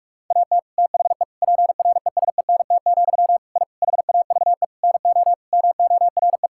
Morse code, heard as a keyed tone, about 41 wpm, 700 Hz. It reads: AT THE PRESENT, I HAVE NO MORE
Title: Sentences from Top 1000 Words - Review Part 5 - 45wpm